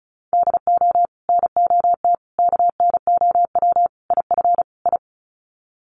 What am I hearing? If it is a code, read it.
Morse code, heard as a keyed tone, about 34 wpm, 700 Hz. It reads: DO NOT KNOW IF I
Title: Sets of 5 Words 35wpm